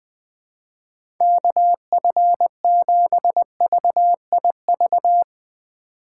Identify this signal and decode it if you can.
Morse code, heard as a keyed tone, about 21 wpm, 700 Hz. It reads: KF7VIV
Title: USA Call Signs - 6 Characters - 2x Character Spacing - 20wpm